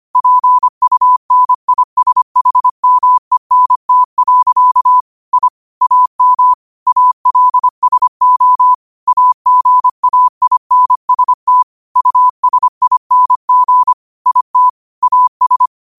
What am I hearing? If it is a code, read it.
Morse code, heard as a keyed tone, about 25 wpm, 1000 Hz. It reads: PUNISHMENT. I AM ALSO AGAINST USING IT AS
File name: qotdfiles / 25wpm